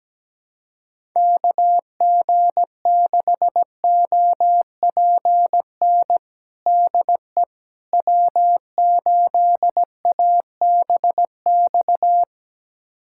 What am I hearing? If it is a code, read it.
Morse code, heard as a keyed tone, about 17 wpm, 700 Hz. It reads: KG6OPN DE W8ABX
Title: USA Callsign DE Callsign 17wpm